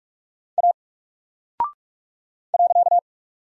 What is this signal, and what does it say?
Morse code, heard as a keyed tone, about 44 wpm, 700 Hz. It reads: A .
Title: Single Character (Letter:Number:Punctuation:Prosign) Alt 45wpm